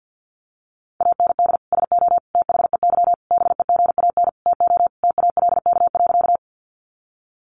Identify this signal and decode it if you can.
Morse code, heard as a keyed tone, about 50 wpm, 700 Hz. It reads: AND SO THEY BEGAN TO TALK.
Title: Sentences from Top 400 Words 50wpm